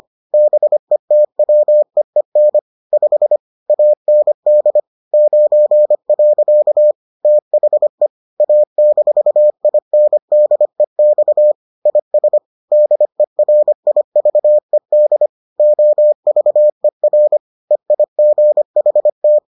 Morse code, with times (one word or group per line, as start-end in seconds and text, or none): 0.3-2.6 BETWEEN
2.9-3.4 5
3.7-4.8 AND
5.1-6.9 9.
7.2-8.1 THE
8.4-11.5 A-INDEX
11.9-12.4 IS
12.7-15.3 DERIVED
15.6-17.4 OVER
17.7-19.4 EIGHT